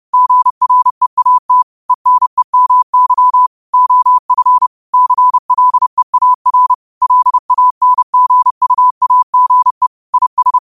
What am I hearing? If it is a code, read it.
Morse code, heard as a keyed tone, about 30 wpm, 1000 Hz. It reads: GREAT ENEMY OF CLEAR LANGUAGE IS
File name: qotdfiles / 30wpm